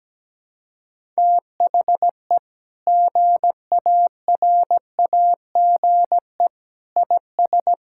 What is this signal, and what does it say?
Morse code, heard as a keyed tone, about 17 wpm, 700 Hz. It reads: THE GARAGE IS
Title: Sentences from Top 1600 Words 17wpm